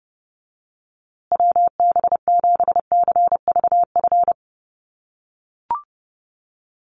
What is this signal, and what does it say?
Morse code, heard as a keyed tone, about 30 wpm, 700 Hz. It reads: WB7CVF
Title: USA Call Signs Repeated 2x 30wpm